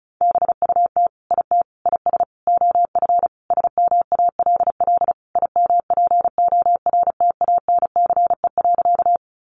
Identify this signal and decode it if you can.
Morse code, heard as a keyed tone, about 34 wpm, 700 Hz. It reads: BUT IT IS OF SMALL IMPORTANCE.
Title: Sentences from Top 1200 Words 35wpm